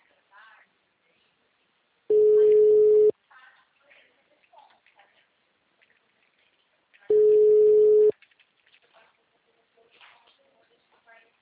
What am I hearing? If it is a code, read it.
Ringback tone (Europe, 425 Hz), with speech.
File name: out- -2013- - - .54624